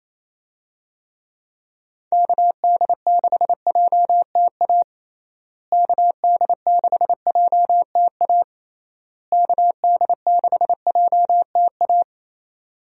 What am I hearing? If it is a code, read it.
Morse code, heard as a keyed tone, about 28 wpm, 700 Hz. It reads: KD6JTA KD6JTA KD6JTA
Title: USA Call Signs - 6 Characters - Repeated 3x - 28wpm